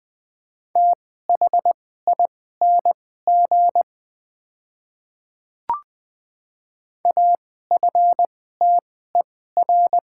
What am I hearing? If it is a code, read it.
Morse code, heard as a keyed tone, about 21 wpm, spaced as for 15 wpm, 690 Hz. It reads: THING AFTER
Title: 2x Character Spacing - Top 200 Words - 20wpm